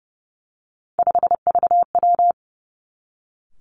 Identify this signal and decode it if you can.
Morse code, heard as a keyed tone, about 30 wpm, 700 Hz. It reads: 5VW
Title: Three Challenging Letter-Number Combinations 30wpm